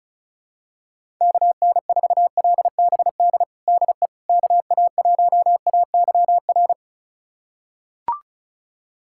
Morse code, heard as a keyed tone, about 34 wpm, 700 Hz. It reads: KN4LBD DE KA1AYR
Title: USA Callsign DE Callsign - 6 Characters - 35wpm